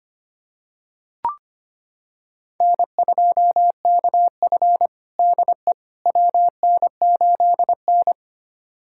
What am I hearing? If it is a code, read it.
Morse code, heard as a keyed tone, about 25 wpm, 700 Hz. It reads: N2KF DE WN8N
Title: USA Callsign DE Callsign - 4 Characters - 25wpm